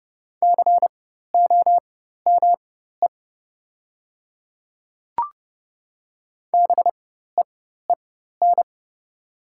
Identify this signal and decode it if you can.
Morse code, heard as a keyed tone, about 30 wpm, spaced as for 14 wpm, 700 Hz. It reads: COME BEEN
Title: 4x Character Spacing - Top 100 Words - 30wpm